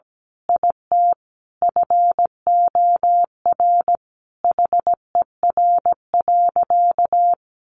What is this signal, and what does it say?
Morse code, heard as a keyed tone, about 17 wpm, 700 Hz. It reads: IT FOR HER.
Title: Sentences from Top 1400 Words 17wpm